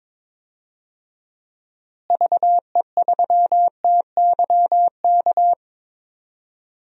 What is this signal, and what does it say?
Morse code, heard as a keyed tone, about 22 wpm, 700 Hz. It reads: VE3TYK
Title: DXCC-100 Call Signs - 6 Characters - Repeated 3x - 22wpm